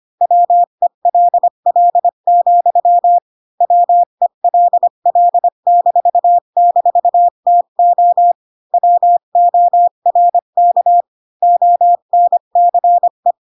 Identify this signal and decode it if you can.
Morse code, heard as a keyed tone, about 25 wpm, 700 Hz. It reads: WELL, WELL--TO WORK ONCE
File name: Pinocchio / Pinocchio ch1 20 25 ik1hge